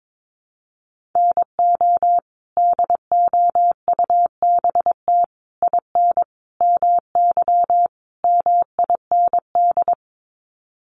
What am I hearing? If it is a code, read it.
Morse code, heard as a keyed tone, about 22 wpm, 700 Hz. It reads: NO DOUBT IN MY MIND
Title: Sets of 5 Words Encore2 22wpm